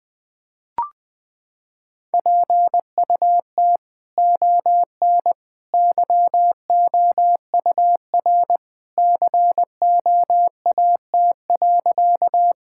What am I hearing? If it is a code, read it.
Morse code, heard as a keyed tone, about 21 wpm, 700 Hz. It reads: PUT ON YOUR COAT.
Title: Sentences from Top 1600 Words 20wpm